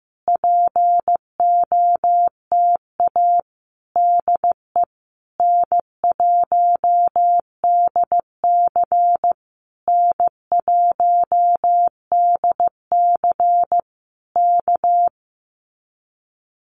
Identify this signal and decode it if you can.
Morse code, heard as a keyed tone, about 15 wpm, 700 Hz. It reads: POTA DE N1DC N1DC K
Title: Common QSO Phrases Encore 15wpm